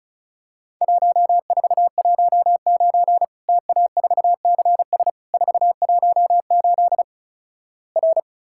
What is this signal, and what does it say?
Morse code, heard as a keyed tone, about 36 wpm, 700 Hz. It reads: 1419 TA4CS 418 R
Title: Worked All Europe DX Contest - QTC Exchange - 35wpm